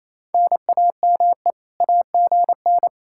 Morse code, heard as a keyed tone, about 36 wpm, 700 Hz. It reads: NAME AGN